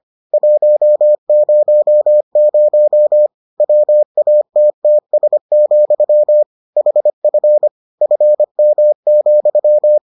Morse code, heard as a keyed tone, about 25 wpm, 600 Hz. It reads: 100 WATTS, HF FM,